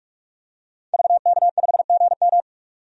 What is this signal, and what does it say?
Morse code, heard as a keyed tone, about 46 wpm, 700 Hz. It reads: VK5GM
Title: DXCC-100 Call Signs - 5 Characters - 45wpm